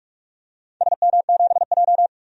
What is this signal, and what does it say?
Morse code, heard as a keyed tone, about 46 wpm, 700 Hz. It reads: SM7J
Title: DXCC-100 Call Signs - 4 Characters - 45wpm